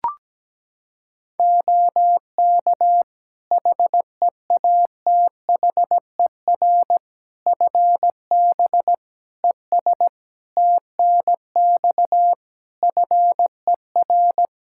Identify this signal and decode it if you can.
Morse code, heard as a keyed tone, about 17 wpm, 700 Hz. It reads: OK HEATHER FB ES TNX FER
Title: K1USN SST QSO Practice 17wpm